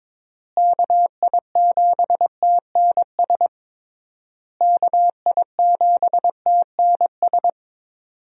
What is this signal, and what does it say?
Morse code, heard as a keyed tone, about 22 wpm, 700 Hz. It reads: KI7TNS KI7TNS
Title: USA Call Signs - 6 Characters - Repeated 3x - 22wpm